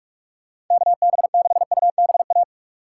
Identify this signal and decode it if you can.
Morse code, heard as a keyed tone, about 44 wpm, 700 Hz. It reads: KB6UBA
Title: USA Call Signs - 6 Characters - 3x Character Spacing - 45wpm